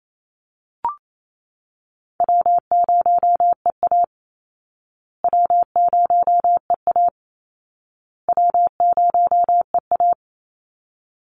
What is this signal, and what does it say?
Morse code, heard as a keyed tone, about 28 wpm, 700 Hz. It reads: W0EA W0EA W0EA
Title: Easy Call Signs Repeated 3x - 28wpm